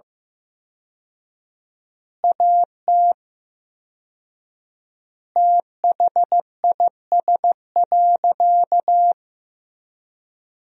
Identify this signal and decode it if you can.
Morse code, heard as a keyed tone, about 15 wpm, 700 Hz. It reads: AT THIS.
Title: 4x Word-Spacing - Sentences from Top 300 Words - 15wpm